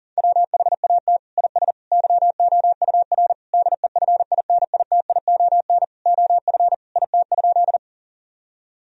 Morse code, heard as a keyed tone, about 37 wpm, 700 Hz. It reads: WHAT IS YOUR DEFINITION OF IT?
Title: Sentences from Top 1000 Words - Review Part 4 - 40wpm